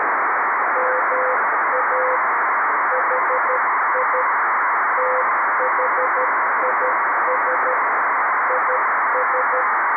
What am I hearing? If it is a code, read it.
Morse code, heard as a keyed tone, about 12 wpm, 510 Hz, with radio noise. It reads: MA HI THIS IS
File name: arissat / recordings / ARISSat 11Sept2011 1354utc 60frames